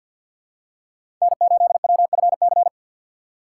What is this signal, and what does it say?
Morse code, heard as a keyed tone, about 48 wpm, 700 Hz. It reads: N8WFC
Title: USA Call Signs - 2x Character Spacing - 50wpm